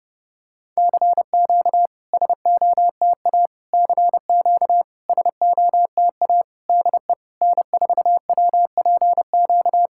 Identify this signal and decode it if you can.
Morse code, heard as a keyed tone, about 31 wpm, 690 Hz. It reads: CQ SOTA CQ SOTA DE N4WPQ
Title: SOTA QSO Practice 30wpm